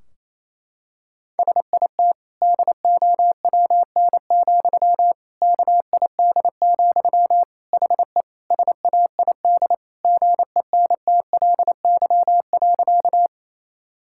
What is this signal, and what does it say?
Morse code, heard as a keyed tone, about 28 wpm, 700 Hz. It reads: SIT DOWN, KID, HE SAID GENTLY.
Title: Sentences from Top 1600 Words 28wpm